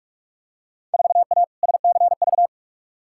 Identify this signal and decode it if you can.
Morse code, heard as a keyed tone, about 46 wpm, 700 Hz. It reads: 4A SCV
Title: Field Day Exchanges Encore - 45wpm